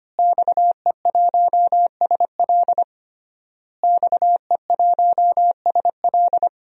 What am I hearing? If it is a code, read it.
Morse code, heard as a keyed tone, about 25 wpm, 700 Hz. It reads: XE1SL XE1SL
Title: DXCC-100 Call Signs - 5 Characters - Repeated 3x - 25wpm